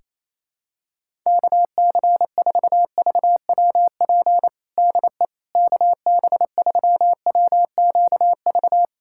Morse code, heard as a keyed tone, about 28 wpm, 700 Hz. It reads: KC4VWP DE KB3WQV
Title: USA Callsign DE Callsign 28wpm